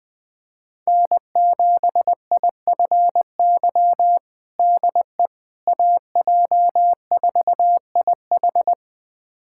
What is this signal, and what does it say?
Morse code, heard as a keyed tone, about 21 wpm, 700 Hz. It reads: N7IFY DE AJ4IH
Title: USA Callsign DE Callsign - 5 Characters - 20wpm